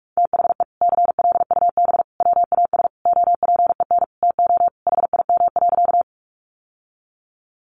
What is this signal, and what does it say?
Morse code, heard as a keyed tone, about 44 wpm, 700 Hz. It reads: THE CLUB WAS OPEN TO HIM.
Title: Sentences from Top 1200 Words 45wpm